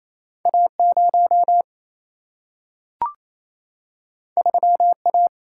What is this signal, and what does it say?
Morse code, heard as a keyed tone, about 28 wpm, 700 Hz. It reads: A0 3A
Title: Single-Letter Single-Number 28wpm